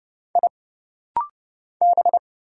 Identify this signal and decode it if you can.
Morse code, heard as a keyed tone, about 29 wpm, 700 Hz. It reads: I B
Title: Single Letters - Rapid-Fire 30wpm